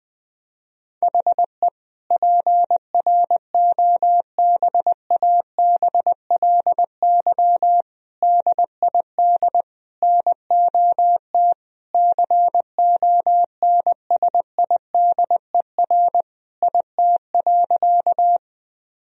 Morse code, heard as a keyed tone, about 21 wpm, 700 Hz. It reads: HE PROBABLY DID NOT CONSIDER IT.
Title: Sentences from Top 1000 Words - Review Part 4 - 20wpm